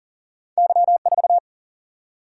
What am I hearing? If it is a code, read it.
Morse code, heard as a keyed tone, about 40 wpm, 700 Hz. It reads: Y4
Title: Single-Letter Single-Number 40wpm